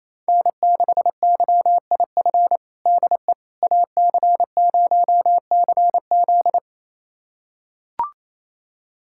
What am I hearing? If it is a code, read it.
Morse code, heard as a keyed tone, about 27 wpm, 700 Hz. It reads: N6YIF DE AC0CZ
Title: USA Callsign DE Callsign - 5 Characters - 28wpm